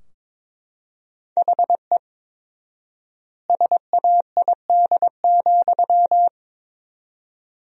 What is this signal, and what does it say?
Morse code, heard as a keyed tone, about 23 wpm, 700 Hz. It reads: HE SAID,
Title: 4x Word-Spacing - Sentences from Top 400 Words - 22wpm